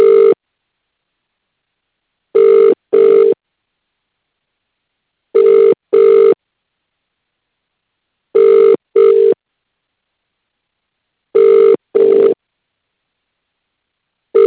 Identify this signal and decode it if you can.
Ringback tone (UK), 400 + 450 Hz.